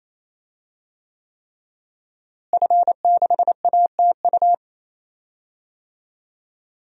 Morse code, heard as a keyed tone, about 27 wpm, 700 Hz. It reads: F6ATU
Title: DXCC-100 Call Signs - 5 Characters - Repeated 3x - 28wpm